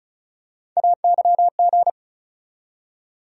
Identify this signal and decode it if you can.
Morse code, heard as a keyed tone, about 36 wpm, 700 Hz. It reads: AYG
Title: USA Call Sign Postfixes - 3 Letters 35wpm